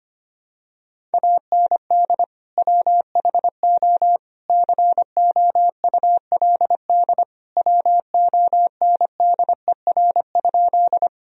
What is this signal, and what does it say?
Morse code, heard as a keyed tone, about 25 wpm, 700 Hz. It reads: AND WHO COULD WONDER?
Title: Sentences from Top 1000 Words - Review Part 3 - 25wpm